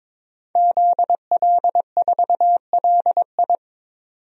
Morse code, heard as a keyed tone, about 23 wpm, 700 Hz. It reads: ZL4LI
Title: DXCC-100 Call Signs - 5 Characters - 22wpm